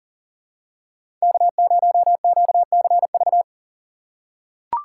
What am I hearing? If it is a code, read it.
Morse code, heard as a keyed tone, about 37 wpm, 700 Hz. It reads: K0QCV
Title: USA Call Signs - 3x Character Spacing - 40wpm